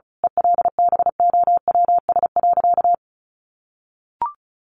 Morse code, heard as a keyed tone, about 34 wpm, 700 Hz. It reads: ELBOWS.
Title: A Sound Of Thunder 35wpm